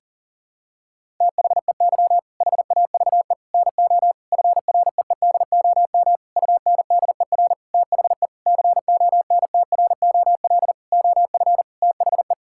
Morse code, heard as a keyed tone, about 40 wpm, 700 Hz. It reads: THEY HAVE NO FREEDOM UNDER THE CONTROL OF THE
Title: Sentences from Top 1000 Words - Review Part 5 - 40wpm